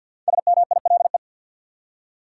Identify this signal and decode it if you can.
Morse code, heard as a keyed tone, about 52 wpm, 700 Hz. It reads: SMILE
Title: Five Letter Words - 50wpm